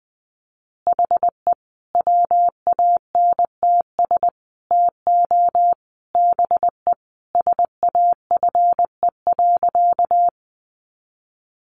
Morse code, heard as a keyed tone, about 20 wpm, 700 Hz. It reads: HE WANTS TO BE SAFE.
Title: Additional Easy Phrases 20wpm